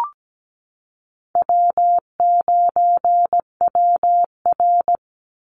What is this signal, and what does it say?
Morse code, heard as a keyed tone, about 17 wpm, 700 Hz. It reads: W9WR
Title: Call Signs Repeated 2x - 17wpm